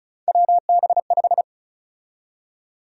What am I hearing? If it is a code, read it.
Morse code, heard as a keyed tone, about 36 wpm, 700 Hz. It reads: WB5
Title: USA Call Sign Prefixes - 35wpm